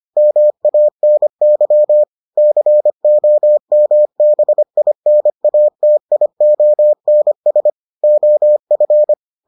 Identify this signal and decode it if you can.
Morse code, heard as a keyed tone, about 25 wpm, 600 Hz. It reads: MANY COMBINATIONS OF